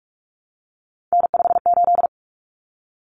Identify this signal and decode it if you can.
Morse code, heard as a keyed tone, about 44 wpm, 700 Hz. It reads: N58
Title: Common Cut Numbers - 3-Digit 45wpm